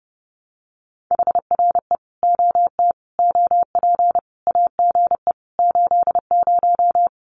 Morse code, heard as a keyed tone, about 30 wpm, 700 Hz. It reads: HRE OT OP AGE 80
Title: Common QSO Phrases 30wpm